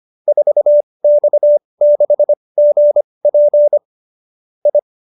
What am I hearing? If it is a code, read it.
Morse code, heard as a keyed tone, about 25 wpm, spaced as for 20 wpm, 570 Hz. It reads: 4X6GP I